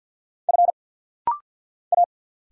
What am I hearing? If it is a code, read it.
Morse code, heard as a keyed tone, about 50 wpm, 690 Hz. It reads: F A
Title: Single Letter-Number Alt - Rapid-Fire 50wpm